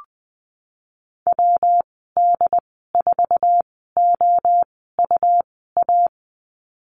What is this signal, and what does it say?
Morse code, heard as a keyed tone, about 20 wpm, spaced as for 15 wpm, 700 Hz. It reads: WD4OUA
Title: 147 Lesson 24g - Letter y - Call Signs 20wpm